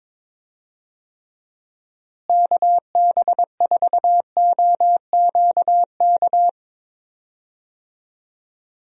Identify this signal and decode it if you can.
Morse code, heard as a keyed tone, about 23 wpm, 700 Hz. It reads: KB4OQK
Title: USA Call Signs - 6 Characters - Repeated 3x - 22wpm